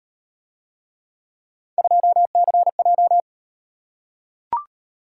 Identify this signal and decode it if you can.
Morse code, heard as a keyed tone, about 38 wpm, 700 Hz. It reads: 2CJ